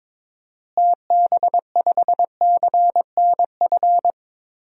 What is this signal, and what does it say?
Morse code, heard as a keyed tone, about 22 wpm, 700 Hz. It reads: TB5CNF
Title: DXCC-100 Call Signs - 6 Characters - Repeated 2x - 22wpm